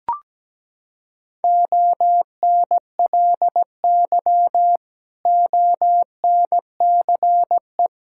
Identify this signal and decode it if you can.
Morse code, heard as a keyed tone, about 17 wpm, 700 Hz. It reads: ONLY ONCE
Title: Sentences from Top 1600 Words 17wpm